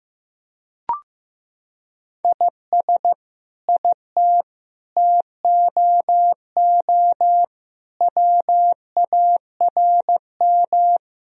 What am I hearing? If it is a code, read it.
Morse code, heard as a keyed tone, about 15 wpm, 700 Hz. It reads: IS IT TOO WARM
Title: Sentences from Top 1000 Words - Review Part 2 - 15wpm